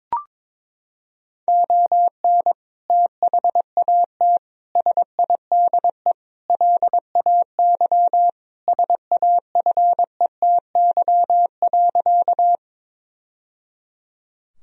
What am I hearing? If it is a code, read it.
Morse code, heard as a keyed tone, about 22 wpm, 700 Hz. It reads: ON THAT SIDE LAY SAFETY.
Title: Sentences from Top 1000 Words - Review Part 4 - 22wpm